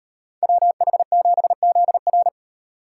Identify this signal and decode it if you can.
Morse code, heard as a keyed tone, about 38 wpm, 700 Hz. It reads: WH7ZR